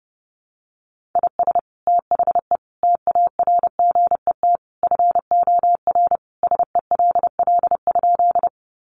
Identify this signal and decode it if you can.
Morse code, heard as a keyed tone, about 30 wpm, 700 Hz. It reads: IS THE TARGET FOR SELL?
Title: Sentences from Top 1200 Words 30wpm